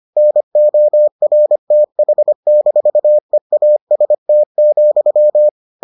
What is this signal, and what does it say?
Morse code, heard as a keyed tone, about 25 wpm, 600 Hz. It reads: NORTH-EAST,